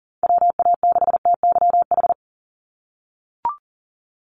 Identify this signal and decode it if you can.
Morse code, heard as a keyed tone, about 41 wpm, 700 Hz. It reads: WA6TYH
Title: USA Call Signs 40wpm